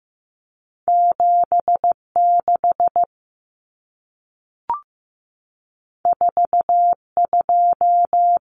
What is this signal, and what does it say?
Morse code, heard as a keyed tone, about 15 wpm, 700 Hz. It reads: 76 42
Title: Numbers - 2-Digit 15wpm